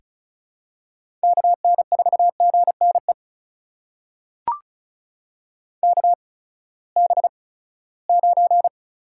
Morse code, heard as a keyed tone, about 36 wpm, spaced as for 20 wpm, 700 Hz. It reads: KN4GNE KB9
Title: USA Call Signs - 6 Characters - 8x Character Spacing - 35wpm